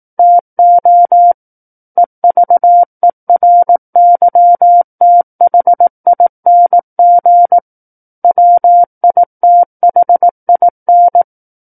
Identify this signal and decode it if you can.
Morse code, heard as a keyed tone, about 18 wpm, 700 Hz. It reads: TO EVERYTHING WITHIN
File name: Book One / 09-Chapter 9 - 18 WPM